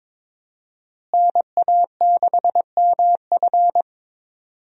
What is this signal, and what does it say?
Morse code, heard as a keyed tone, about 22 wpm, 700 Hz. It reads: NA6MF
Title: USA Call Signs - 5 Characters - Repeated 2x - 22wpm